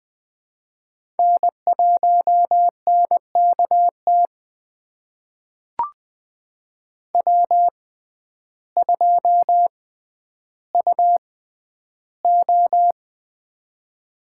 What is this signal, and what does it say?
Morse code, heard as a keyed tone, about 20 wpm, spaced as for 11 wpm, 700 Hz. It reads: N1NKT W2UO
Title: USA Call Signs - 6x Character Spacing - 20wpm